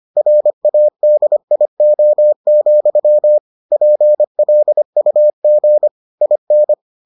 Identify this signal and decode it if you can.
Morse code, heard as a keyed tone, about 25 wpm, 600 Hz. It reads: RADIO, PLUG IN